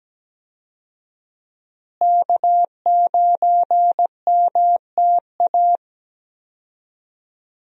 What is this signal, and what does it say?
Morse code, heard as a keyed tone, about 17 wpm, 700 Hz. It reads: K9MTA
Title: USA Call Signs - 5 Characters - Repeated 2x - 17wpm